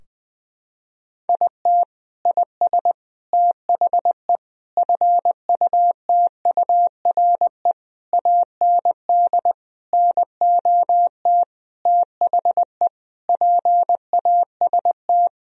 Morse code, heard as a keyed tone, about 21 wpm, 700 Hz. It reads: IT IS THE FUTURE AND NOT THE PAST
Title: Sentences from Top 1000 Words - Review Part 4 - 20wpm